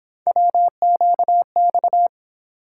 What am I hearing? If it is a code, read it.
Morse code, heard as a keyed tone, about 28 wpm, 700 Hz. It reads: WQX